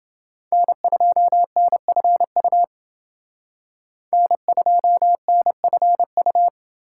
Morse code, heard as a keyed tone, about 29 wpm, 700 Hz. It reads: N2NFU N2NFU